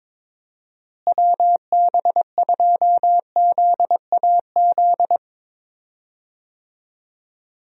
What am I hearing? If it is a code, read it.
Morse code, heard as a keyed tone, about 22 wpm, 700 Hz. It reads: WB2ZAZ
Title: USA Call Signs - 6 Characters - Repeated 3x - 22wpm